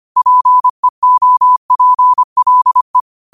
Morse code, heard as a keyed tone, about 25 wpm, 1000 Hz. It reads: PEOPLE
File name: qotdfiles / 25wpm